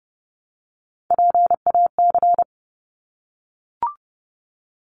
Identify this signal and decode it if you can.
Morse code, heard as a keyed tone, about 30 wpm, 700 Hz. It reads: PAC
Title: ARRL Sections 30wpm